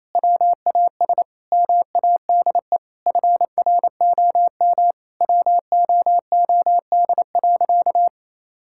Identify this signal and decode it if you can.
Morse code, heard as a keyed tone, about 27 wpm, 700 Hz. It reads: WAS MADE FROM WOOD.
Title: Sentences from Top 1000 Words - Review Part 6 - 28wpm